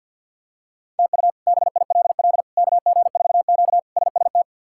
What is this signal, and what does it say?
Morse code, heard as a keyed tone, about 52 wpm, 700 Hz. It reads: TU BILL KG4Q SST
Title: K1USN SST QSO Practice - 50wpm